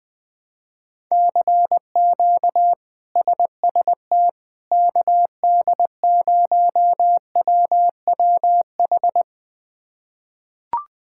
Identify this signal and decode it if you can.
Morse code, heard as a keyed tone, about 21 wpm, 700 Hz. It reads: CQ SST KD0WWH
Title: K1USN SST QSO Practice - 20wpm